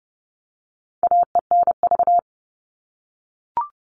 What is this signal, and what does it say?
Morse code, heard as a keyed tone, about 30 wpm, 700 Hz. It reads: AENV
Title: All Cut Numbers - 4-Digit 30wpm